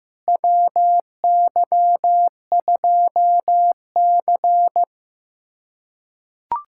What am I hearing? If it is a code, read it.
Morse code, heard as a keyed tone, about 16 wpm, 700 Hz. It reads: WY2C
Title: USA Callsign DE Callsign - 4 Characters - 15wpm